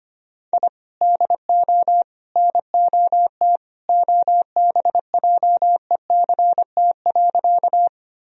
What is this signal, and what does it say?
Morse code, heard as a keyed tone, about 25 wpm, 700 Hz. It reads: I DO NOT OBJECT.
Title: Sentences from Top 1000 Words - Review Part 3 - 25wpm